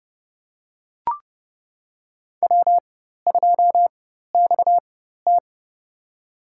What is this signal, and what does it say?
Morse code, heard as a keyed tone, about 30 wpm, spaced as for 14 wpm, 700 Hz. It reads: W2XT
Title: USA Call Signs - 4 Characters - 4x Character Spacing - 30wpm